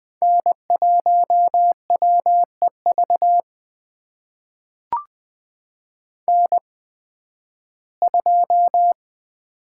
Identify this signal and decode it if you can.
Morse code, heard as a keyed tone, about 20 wpm, spaced as for 13 wpm, 700 Hz. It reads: N1WEV N2
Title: USA Call Signs - 5 Characters - 8x Character Spacing - 20wpm